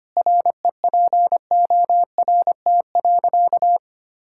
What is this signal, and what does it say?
Morse code, heard as a keyed tone, about 25 wpm, 700 Hz. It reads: REPORT.
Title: Sentences from Top 1000 Words - Review Part 6 - 25wpm